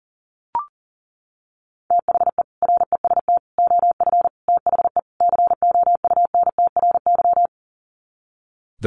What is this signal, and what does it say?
Morse code, heard as a keyed tone, about 37 wpm, 700 Hz. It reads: THE REST OF THE COUNTRY
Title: Sets of 5 Words 40wpm